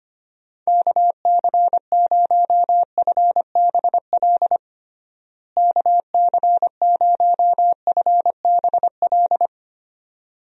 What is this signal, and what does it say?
Morse code, heard as a keyed tone, about 25 wpm, 700 Hz. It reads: KC0FBL KC0FBL
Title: USA Call Signs - 6 Characters - Repeated 2x - 25wpm